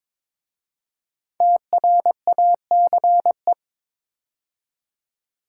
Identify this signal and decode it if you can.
Morse code, heard as a keyed tone, about 22 wpm, 700 Hz. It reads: TRACE
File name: 5 Letter Words - First 3 Letters Given - 22wpm